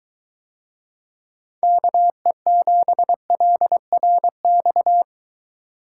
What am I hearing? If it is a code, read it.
Morse code, heard as a keyed tone, about 22 wpm, 700 Hz. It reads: KE7LRX